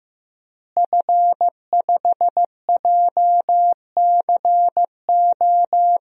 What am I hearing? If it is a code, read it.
Morse code, heard as a keyed tone, about 16 wpm, 700 Hz. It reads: F5JCO
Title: DXCC-100 Call Signs - 5 Characters - Repeated 3x - 15wpm